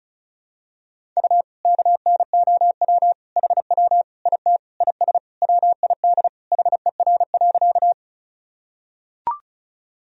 Morse code, heard as a keyed tone, about 34 wpm, 700 Hz. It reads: U KNOW HW IT IS WID HER.
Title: Sentences from Top 100 Words and Abbreviations 35wpm